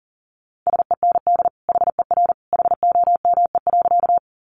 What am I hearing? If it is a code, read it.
Morse code, heard as a keyed tone, about 40 wpm, 700 Hz. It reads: SEND HER HOME.
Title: Sentences from Top 1500 Words 40wpm